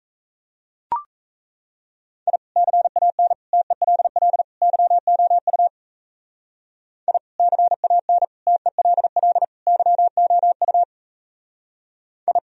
Morse code, heard as a keyed tone, about 35 wpm, 700 Hz. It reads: I CAN TELL YOU I CAN TELL YOU I
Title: Speed-Racing - Sets of 4 Words 28wpm